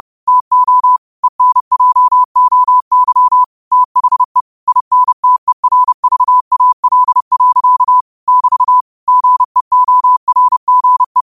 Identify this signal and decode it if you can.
Morse code, heard as a keyed tone, about 30 wpm, 1000 Hz. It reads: TO ENJOY THE INTERVAL. <BT> GEORGE
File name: qotdfiles / 30wpm